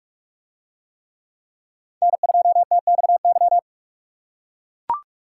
Morse code, heard as a keyed tone, about 44 wpm, 700 Hz. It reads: N2TXY